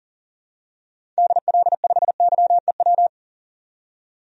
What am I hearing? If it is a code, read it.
Morse code, heard as a keyed tone, about 40 wpm, 700 Hz. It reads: DL5YEW
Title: DXCC-100 Call Signs - 6 Characters - 40wpm